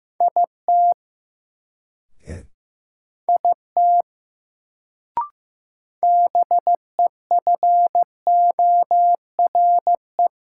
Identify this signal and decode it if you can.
Morse code, heard as a keyed tone, about 15 wpm, 700 Hz. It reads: IT IT BEFORE
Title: Sets of 2 Words - New Word Drill - 15wpm